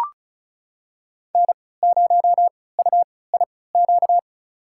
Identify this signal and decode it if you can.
Morse code, heard as a keyed tone, about 34 wpm, spaced as for 20 wpm, 700 Hz. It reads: N0UIQ
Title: USA Call Signs - 5 Characters - 3x Character Spacing - 35wpm